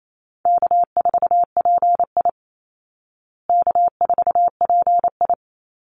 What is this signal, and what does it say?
Morse code, heard as a keyed tone, about 28 wpm, 700 Hz. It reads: K4PI K4PI
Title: Easy Call Signs Repeated 2x - 28wpm